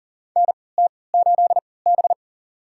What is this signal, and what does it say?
Morse code, heard as a keyed tone, about 40 wpm, spaced as for 23 wpm, 700 Hz. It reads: NT8B
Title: USA Call Signs - 4 Characters - 3x Character Spacing - 40wpm